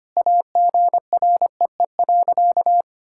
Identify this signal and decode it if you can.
Morse code, heard as a keyed tone, about 25 wpm, 700 Hz. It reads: AGREE.
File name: practice / 25wpm / Sentences with 4-Word Phrases - 25wpm